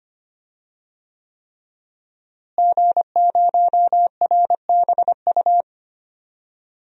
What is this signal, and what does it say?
Morse code, heard as a keyed tone, about 25 wpm, 700 Hz. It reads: G0RBU
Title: DXCC-100 Call Signs - 5 Characters - Repeated 3x - 25wpm